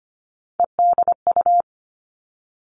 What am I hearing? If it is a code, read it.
Morse code, heard as a keyed tone, about 25 wpm, 700 Hz. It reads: EDU
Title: Three Challenging Letter-Number Combinations 25wpm